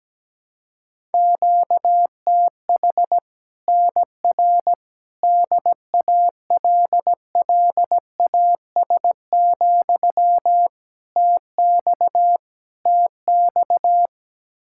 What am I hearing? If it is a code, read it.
Morse code, heard as a keyed tone, about 17 wpm, 700 Hz. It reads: QTH NR DALLAS, TX TX
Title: Practice QSO - Exchange 1 - Beginning - 17wpm